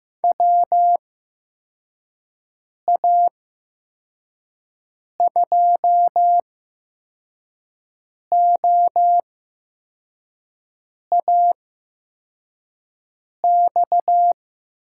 Morse code, heard as a keyed tone, about 15 wpm, spaced as for 4 wpm, 700 Hz. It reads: WA2OAX
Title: USA Call Signs - 6 Characters - 8x Character Spacing - 15wpm